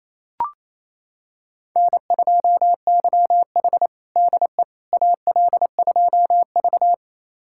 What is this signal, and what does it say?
Morse code, heard as a keyed tone, about 28 wpm, 700 Hz. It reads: N2YH DE AL2V
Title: USA Callsign DE Callsign - 4 Characters - 28wpm